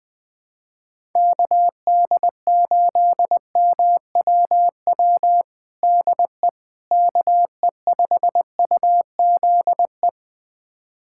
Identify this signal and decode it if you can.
Morse code, heard as a keyed tone, about 20 wpm, 700 Hz. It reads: KD8MWW DE KE5UZE
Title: USA Callsign DE Callsign - 6 Characters - 20wpm